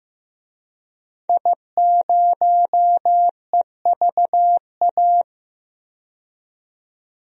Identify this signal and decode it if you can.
Morse code, heard as a keyed tone, about 15 wpm, 700 Hz. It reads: I0EVA
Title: DXCC-100 Call Signs - 5 Characters - 15wpm